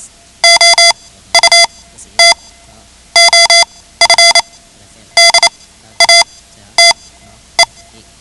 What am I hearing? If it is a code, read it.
Morse code, heard as a keyed tone, about 28 wpm, spaced as for 13 wpm, 2050 Hz. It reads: OUT OF DATE
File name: ham radio / CodeQuick / Code Quick 10